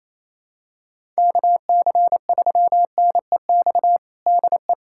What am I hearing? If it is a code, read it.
Morse code, heard as a keyed tone, about 28 wpm, 700 Hz. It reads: KC3NEX DE
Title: USA Callsign DE Callsign - 6 Characters - 28wpm